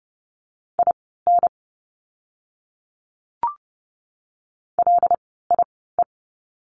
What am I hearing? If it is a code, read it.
Morse code, heard as a keyed tone, about 30 wpm, spaced as for 17 wpm, 700 Hz. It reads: IN LIE
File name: Lesson 16f - Letter c - Words 30wpm